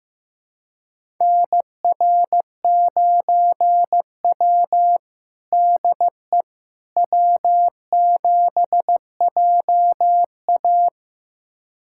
Morse code, heard as a keyed tone, about 15 wpm, 700 Hz. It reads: NR9W DE W7JA
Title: USA Callsign DE Callsign - 4 Characters - 15wpm